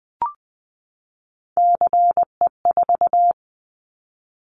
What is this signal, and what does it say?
Morse code, heard as a keyed tone, about 20 wpm, 700 Hz. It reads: CE4
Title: Callsign Prefixes for Top 100 DXCC - 20wpm